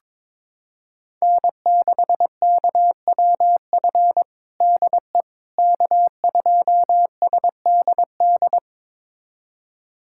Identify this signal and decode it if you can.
Morse code, heard as a keyed tone, about 22 wpm, 700 Hz. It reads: N6KWF DE K2SDD
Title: USA Callsign DE Callsign - 5 Characters - 22wpm